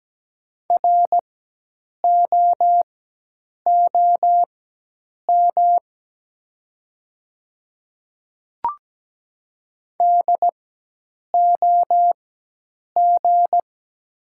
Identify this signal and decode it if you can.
Morse code, heard as a keyed tone, about 17 wpm, spaced as for 8 wpm, 700 Hz. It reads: ROOM DOG
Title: 4x Character Spacing - Top 500 Words - 17wpm